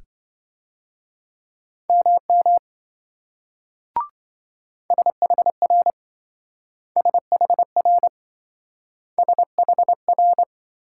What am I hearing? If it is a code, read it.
Morse code, heard as a keyed tone, about 26 wpm, 700 Hz. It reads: MM SHR SHR SHR